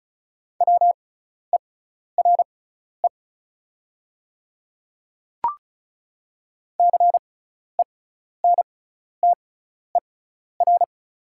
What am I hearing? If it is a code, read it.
Morse code, heard as a keyed tone, about 34 wpm, spaced as for 12 wpm, 700 Hz. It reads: WERE CENTER
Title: 6x Character Spacing - Top 500 Words - 35wpm